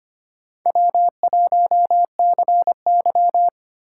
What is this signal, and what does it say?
Morse code, heard as a keyed tone, about 25 wpm, 700 Hz. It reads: W1CY
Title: USA Call Signs - 4 Characters - 8x Character Spacing - 25wpm